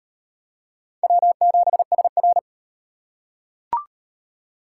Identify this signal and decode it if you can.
Morse code, heard as a keyed tone, about 38 wpm, 700 Hz. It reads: W7SR